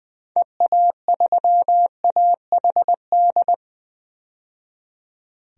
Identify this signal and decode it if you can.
Morse code, heard as a keyed tone, about 20 wpm, 700 Hz. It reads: EA3AHD
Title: DXCC-100 Call Signs - 6 Characters - 20wpm